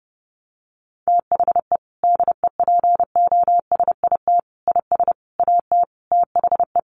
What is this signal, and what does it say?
Morse code, heard as a keyed tone, about 30 wpm, 700 Hz. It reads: THE DEPOSIT IS AT THE
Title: Sentences from Top 1300 Words 30wpm